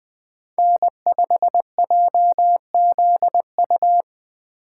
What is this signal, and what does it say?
Morse code, heard as a keyed tone, about 19 wpm, 700 Hz. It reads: N5JZU
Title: Speed Racing - USA 1x3 Call Signs 17wpm